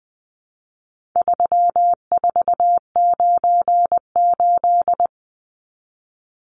Morse code, heard as a keyed tone, about 20 wpm, 700 Hz. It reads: 3498
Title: Numbers - 4-Digit 20wpm